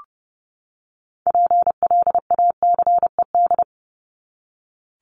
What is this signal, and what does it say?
Morse code, heard as a keyed tone, about 30 wpm, 700 Hz. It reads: PLACED
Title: ED Suffix - Top 100 Words with Base Words 30wpm